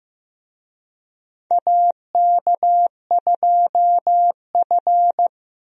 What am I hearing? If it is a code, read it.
Morse code, heard as a keyed tone, about 15 wpm, 700 Hz. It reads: AK2F
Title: USA Call Signs - 4 Characters - 15wpm